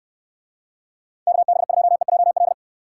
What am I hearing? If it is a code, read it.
Morse code, heard as a keyed tone, about 69 wpm, 700 Hz. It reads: KB2EJL